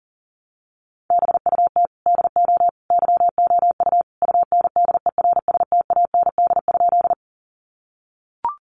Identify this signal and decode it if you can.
Morse code, heard as a keyed tone, about 40 wpm, 700 Hz. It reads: BUT DO YOU UNDERSTAND?
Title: Sentences from Top 600 Words 40wpm